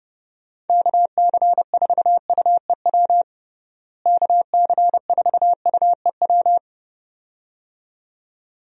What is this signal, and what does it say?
Morse code, heard as a keyed tone, about 32 wpm, 700 Hz. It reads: KC4UEW KC4UEW
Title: USA Call Signs - 6 Characters - Repeated 2x - 30wpm